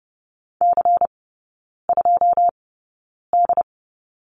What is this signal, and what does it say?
Morse code, heard as a keyed tone, about 30 wpm, spaced as for 9 wpm, 700 Hz. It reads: C2D
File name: Lesson 16e - Letter c - Three Characters 30wpm